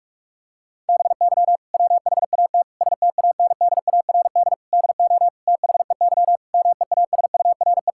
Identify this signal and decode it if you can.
Morse code, heard as a keyed tone, about 41 wpm, 700 Hz. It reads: BY WHAT STANDARD DO THEY MEASURE
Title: Sentences from Top 1000 Words - Review Part 6 - 45wpm